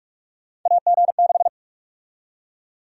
Morse code, heard as a keyed tone, about 42 wpm, 700 Hz. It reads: AG6
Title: USA Call Sign Prefixes - 45wpm